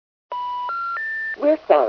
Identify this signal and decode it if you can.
Special information tone, with speech.